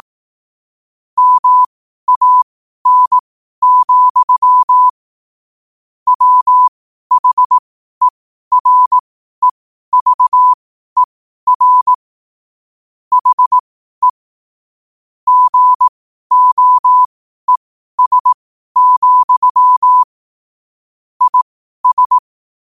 Morse code, heard as a keyed tone, about 18 wpm, spaced as for 12 wpm, 1000 Hz. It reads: MAN, WHEREVER HE GOES, IS
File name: qotdfiles / 12wpm